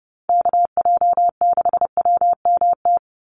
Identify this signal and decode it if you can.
Morse code, heard as a keyed tone, about 30 wpm, 700 Hz. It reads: KJ6WMT
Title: USA Call Signs 30wpm